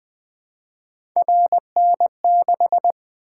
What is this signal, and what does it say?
Morse code, heard as a keyed tone, about 20 wpm, 700 Hz. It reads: RN6
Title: Three Letter-Number Combinations - 20wpm